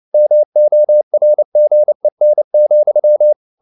Morse code, heard as a keyed tone, about 29 wpm, 600 Hz. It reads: MORGEN,